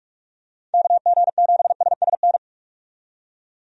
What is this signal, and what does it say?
Morse code, heard as a keyed tone, about 46 wpm, 700 Hz. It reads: KG7SSN
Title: USA Call Signs - 6 Characters - 45wpm